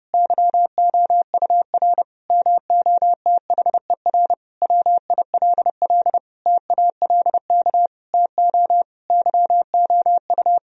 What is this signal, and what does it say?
Morse code, heard as a keyed tone, about 32 wpm, 700 Hz. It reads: YOUR MOTHER WILL TALK TO YOU
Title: Sentences from Top 1000 Words - Review Part 1 - 30wpm